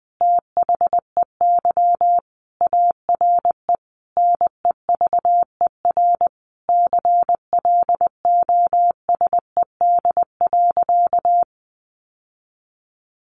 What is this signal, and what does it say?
Morse code, heard as a keyed tone, about 20 wpm, 700 Hz. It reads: THEY ARE NEVER CLOSED.
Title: Sentences from Top 1400 Words 20wpm